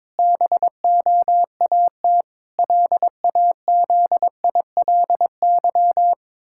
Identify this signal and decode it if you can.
Morse code, heard as a keyed tone, about 22 wpm, 700 Hz. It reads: BOAT LAZILY
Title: Pangram Challenge v2 - 22wpm